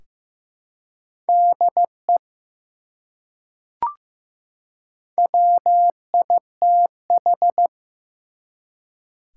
Morse code, heard as a keyed tone, about 15 wpm, 700 Hz. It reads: DE WITH
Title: Top 180 QSO Elements 15wpm